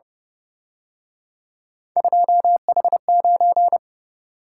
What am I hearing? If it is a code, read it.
Morse code, heard as a keyed tone, about 31 wpm, 700 Hz. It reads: 2H9
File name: Speed-Racing - Three Letter-Number Combinations - 25wpm